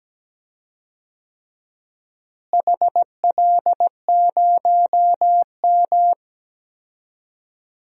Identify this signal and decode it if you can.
Morse code, heard as a keyed tone, about 17 wpm, 700 Hz. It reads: HL0M
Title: DXCC-100 Call Signs - 4 Characters - Repeated 3x - 17wpm